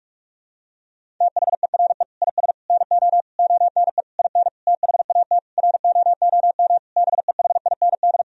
Morse code, heard as a keyed tone, about 41 wpm, 700 Hz. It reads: THERE IS NO ONE IN THAT ROOM BEHIND
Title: Sentences from Top 500 Words and Abbreviations 45wpm